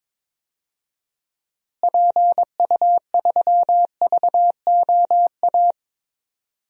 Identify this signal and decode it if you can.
Morse code, heard as a keyed tone, about 21 wpm, 700 Hz. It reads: PU3VOA
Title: DXCC-100 Call Signs - 6 Characters - Repeated 3x - 22wpm